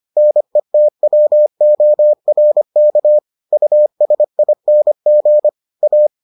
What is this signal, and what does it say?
Morse code, heard as a keyed tone, about 25 wpm, 600 Hz. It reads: NETWORK USING A